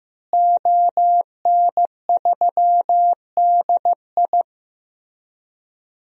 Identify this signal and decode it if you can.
Morse code, heard as a keyed tone, about 15 wpm, 700 Hz. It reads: ON3DI
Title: DXCC-100 Call Signs - 5 Characters - Repeated 3x - 15wpm